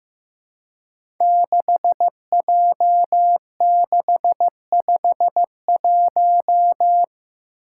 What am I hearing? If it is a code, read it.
Morse code, heard as a keyed tone, about 15 wpm, 700 Hz. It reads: 6J651
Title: Five Challenging Letter-Number Combinations 15wpm